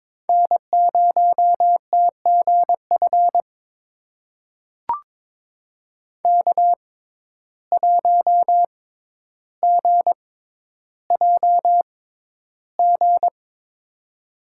Morse code, heard as a keyed tone, about 22 wpm, spaced as for 11 wpm, 700 Hz. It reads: N0TGF K1GJG
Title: USA Call Signs - 5 Characters - 6x Character Spacing - 22wpm